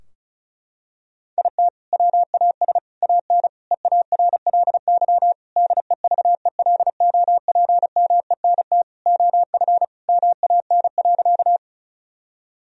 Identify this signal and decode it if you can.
Morse code, heard as a keyed tone, about 34 wpm, 700 Hz. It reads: IT WAS AN EARLY DEVELOPMENT OF MAN.
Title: Sentences from Top 1000 Words - Review Part 4 - 35wpm